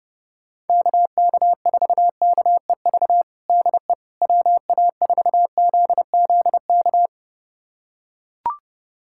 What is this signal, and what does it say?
Morse code, heard as a keyed tone, about 30 wpm, 700 Hz. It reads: KK4KEV DE WA4ZZK
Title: USA Callsign DE Callsign - 6 Characters - 30wpm